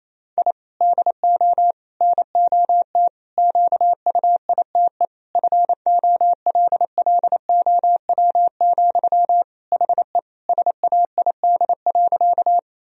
Morse code, heard as a keyed tone, about 28 wpm, 700 Hz. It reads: I DO NOT QUITE FOLLOW, HE SAID.
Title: Sentences from Top 1000 Words - Review Part 1 - 28wpm